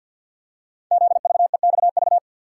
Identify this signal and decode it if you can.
Morse code, heard as a keyed tone, about 50 wpm, 700 Hz. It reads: ZVEXV
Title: Code Groups - 5 Letters - 50wpm